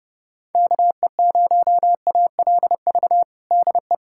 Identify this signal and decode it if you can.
Morse code, heard as a keyed tone, about 32 wpm, 700 Hz. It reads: KE0ALV DE
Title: USA Callsign DE Callsign - 6 Characters - 30wpm